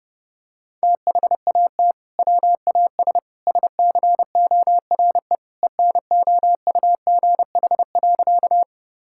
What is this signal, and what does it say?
Morse code, heard as a keyed tone, about 30 wpm, 700 Hz. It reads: THAT WAS SCORE ENOUGH.
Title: Sentences from Top 1600 Words 30wpm